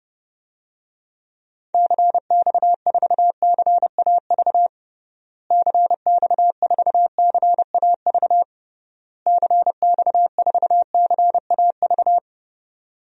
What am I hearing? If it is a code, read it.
Morse code, heard as a keyed tone, about 30 wpm, 700 Hz. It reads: CX4CAV CX4CAV CX4CAV
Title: DXCC-100 Call Signs - 6 Characters - Repeated 3x - 30wpm